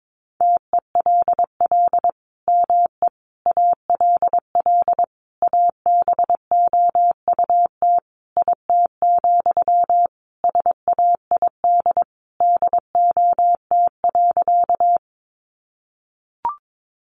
Morse code, heard as a keyed tone, about 22 wpm, 700 Hz. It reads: TELL ME ALL ABOUT IT, SAID DOT.
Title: Sentences from Top 1500 Words 22wpm